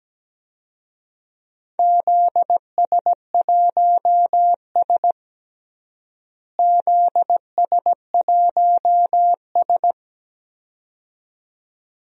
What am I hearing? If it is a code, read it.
Morse code, heard as a keyed tone, about 17 wpm, 700 Hz. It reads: ZS1S ZS1S
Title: DXCC-100 Call Signs - 4 Characters - Repeated 2x - 17wpm